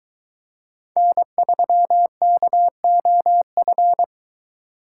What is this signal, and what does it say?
Morse code, heard as a keyed tone, about 22 wpm, 700 Hz. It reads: N3KOF